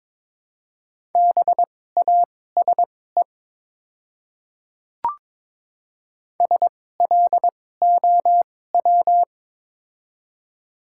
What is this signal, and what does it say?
Morse code, heard as a keyed tone, about 22 wpm, spaced as for 16 wpm, 700 Hz. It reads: BASE SLOW
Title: 2x Character Spacing - Top 500 Words - 22wpm